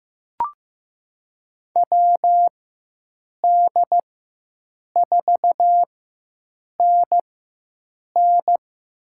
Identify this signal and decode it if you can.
Morse code, heard as a keyed tone, about 15 wpm, spaced as for 7 wpm, 700 Hz. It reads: WD4NN
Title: USA Call Signs - 4x Character Spacing - 15wpm